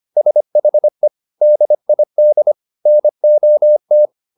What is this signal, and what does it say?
Morse code, heard as a keyed tone, about 25 wpm, 600 Hz. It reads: SHE DID NOT